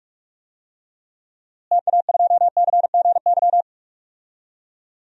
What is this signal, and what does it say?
Morse code, heard as a keyed tone, about 44 wpm, 700 Hz. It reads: TA2CGY
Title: DXCC-100 Call Signs - 6 Characters - Repeated 2x - 45wpm